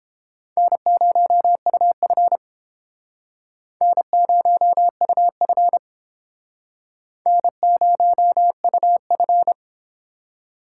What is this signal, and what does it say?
Morse code, heard as a keyed tone, about 30 wpm, 700 Hz. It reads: N0UF N0UF N0UF